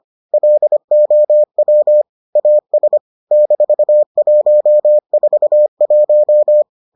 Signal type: Morse code

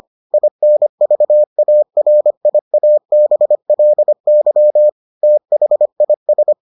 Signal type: Morse code

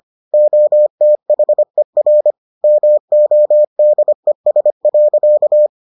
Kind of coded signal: Morse code

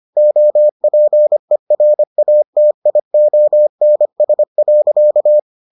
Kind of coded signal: Morse code

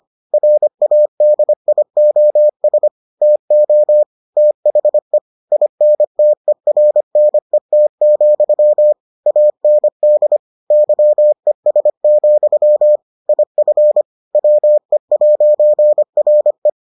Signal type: Morse code

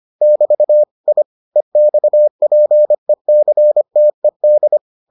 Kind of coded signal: Morse code

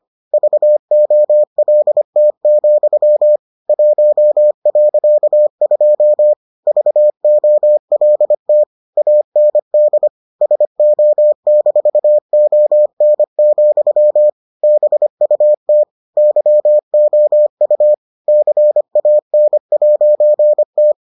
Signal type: Morse code

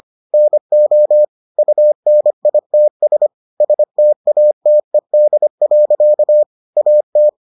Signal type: Morse code

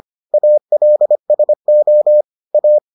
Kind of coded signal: Morse code

